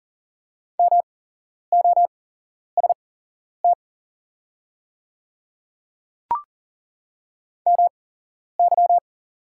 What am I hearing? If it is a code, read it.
Morse code, heard as a keyed tone, about 40 wpm, spaced as for 11 wpm, 700 Hz. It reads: MOST MY
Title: 8x Character Spacing - Top 100 Words - 40wpm